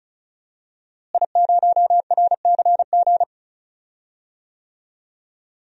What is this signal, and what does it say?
Morse code, heard as a keyed tone, about 34 wpm, 700 Hz. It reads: I0RCG
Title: DXCC-100 Call Signs - 5 Characters - Repeated 2x - 35wpm